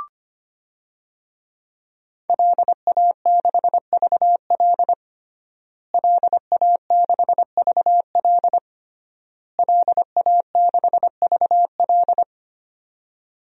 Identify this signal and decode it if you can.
Morse code, heard as a keyed tone, about 25 wpm, 700 Hz. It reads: LA6VL LA6VL LA6VL
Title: DXCC-100 Call Signs - 5 Characters - Repeated 3x - 25wpm